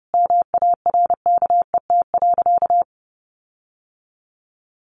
Morse code, heard as a keyed tone, about 30 wpm, 700 Hz. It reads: MARKET.
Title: Sentences from Top 1400 Words 30wpm